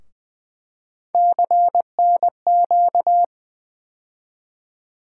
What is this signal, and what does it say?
Morse code, heard as a keyed tone, about 20 wpm, 700 Hz. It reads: CNQ
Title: USA Call Sign Postfixes - 3 Letters 20wpm